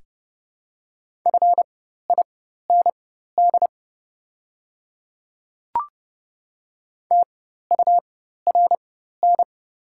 Morse code, heard as a keyed tone, about 30 wpm, spaced as for 14 wpm, 700 Hz. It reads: FIND TURN
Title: 4x Character Spacing - Top 300 Words - 30wpm